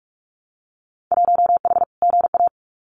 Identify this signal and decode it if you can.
Morse code, heard as a keyed tone, about 46 wpm, 700 Hz. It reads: 1H GA
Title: Winter Field Day Exchanges 45wpm